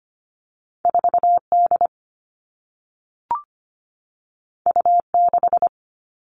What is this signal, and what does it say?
Morse code, heard as a keyed tone, about 25 wpm, 700 Hz. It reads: 4 D U 6
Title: Two Challenging Letter-Number Combinations 25wpm